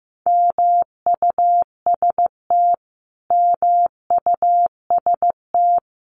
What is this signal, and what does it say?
Morse code, heard as a keyed tone, about 15 wpm, 700 Hz. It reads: MUST MUST
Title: Top 200 Words Repeated 15wpm